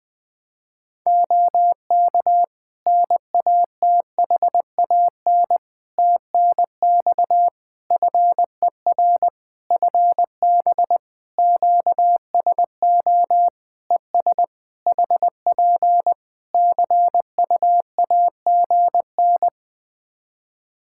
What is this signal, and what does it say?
Morse code, heard as a keyed tone, about 21 wpm, 700 Hz. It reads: OK NATHAN TNX FER FB QSO ES HP CUAGN
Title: Practice QSO - Exchange 4 - Ending - 20wpm